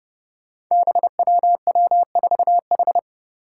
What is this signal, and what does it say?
Morse code, heard as a keyed tone, about 30 wpm, 700 Hz. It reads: BWW4H
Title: Five Challenging Letter-Number Combinations 30wpm